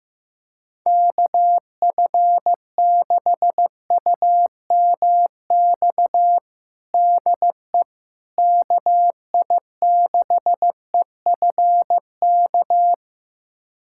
Morse code, heard as a keyed tone, about 15 wpm, 700 Hz. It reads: KF6UMX DE KI6EFK
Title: USA Callsign DE Callsign - 6 Characters - 15wpm